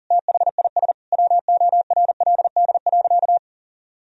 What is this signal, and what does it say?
Morse code, heard as a keyed tone, about 40 wpm, 700 Hz. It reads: THIS WORLD.
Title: Sentences from Top 1000 Words - Review Part 1 - 40wpm